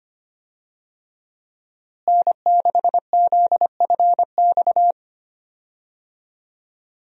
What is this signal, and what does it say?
Morse code, heard as a keyed tone, about 25 wpm, 700 Hz. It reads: N6ZFX
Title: USA Call Signs - 5 Characters - Repeated 3x - 25wpm